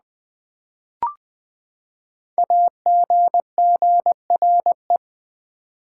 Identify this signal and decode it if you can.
Morse code, heard as a keyed tone, about 20 wpm, 700 Hz. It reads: AGGRE
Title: Build A Word - 5 Letters At A Time - 10-Letter Words - 20wpm